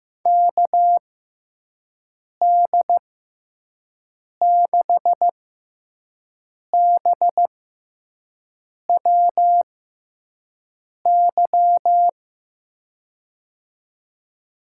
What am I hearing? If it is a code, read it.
Morse code, heard as a keyed tone, about 15 wpm, spaced as for 5 wpm, 700 Hz. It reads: KD6BWY
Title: USA Call Signs - 6 Characters - 6x Character Spacing - 15wpm